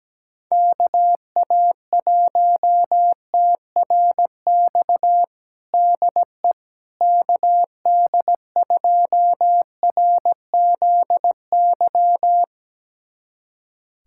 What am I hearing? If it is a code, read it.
Morse code, heard as a keyed tone, about 17 wpm, 700 Hz. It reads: KA1TRX DE KD2RZY
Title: USA Callsign DE Callsign - 6 Characters - 17wpm